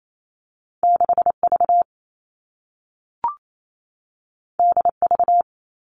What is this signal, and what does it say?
Morse code, heard as a keyed tone, about 28 wpm, 700 Hz. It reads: 6V DV
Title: All Cut Numbers - 2-Digit 28wpm